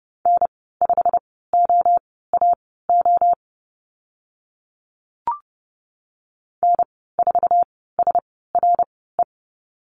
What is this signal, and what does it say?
Morse code, heard as a keyed tone, about 30 wpm, spaced as for 17 wpm, 700 Hz. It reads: N5OAO N4SRE
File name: Lesson 15g - Number 5 - Call Signs 30wpm